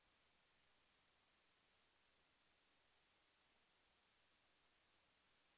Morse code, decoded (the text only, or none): none